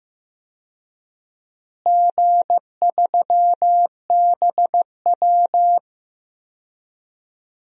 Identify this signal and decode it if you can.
Morse code, heard as a keyed tone, about 15 wpm, 700 Hz. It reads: G3BW
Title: DXCC-100 Call Signs - 4 Characters - Repeated 3x - 15wpm